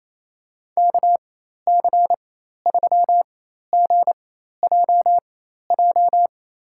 Morse code, heard as a keyed tone, about 28 wpm, spaced as for 13 wpm, 700 Hz. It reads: KC3GJJ
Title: USA Call Signs - 6 Characters - 4x Character Spacing - 28wpm